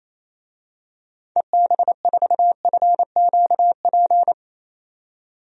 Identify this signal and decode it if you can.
Morse code, heard as a keyed tone, about 27 wpm, 700 Hz. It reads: EB4FQP
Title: DXCC-100 Call Signs - 6 Characters - 28wpm